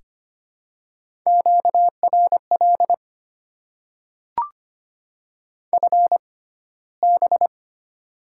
Morse code, heard as a keyed tone, about 25 wpm, spaced as for 15 wpm, 700 Hz. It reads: QRL FB
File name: Top 20 QSO Elements - 6x Character Spacing - 25wpm